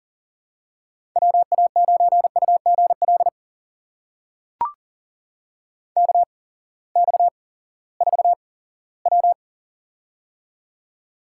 Morse code, heard as a keyed tone, about 39 wpm, spaced as for 20 wpm, 700 Hz. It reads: WA9UGL KX4W
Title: USA Call Signs - 8x Character Spacing - 40wpm